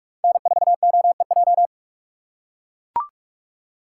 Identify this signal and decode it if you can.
Morse code, heard as a keyed tone, about 42 wpm, 700 Hz. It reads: N4OEJ
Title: USA Call Signs - 5 Characters - 6x Character Spacing - 45wpm